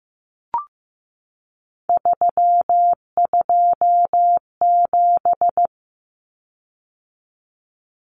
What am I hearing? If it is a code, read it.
Morse code, heard as a keyed tone, about 15 wpm, 700 Hz. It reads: 327
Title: Common Cut Numbers - 3-Digit 15wpm